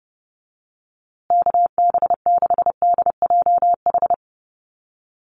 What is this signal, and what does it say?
Morse code, heard as a keyed tone, about 30 wpm, 700 Hz. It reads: KB6DJH
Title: USA Call Signs Repeated 3x 30wpm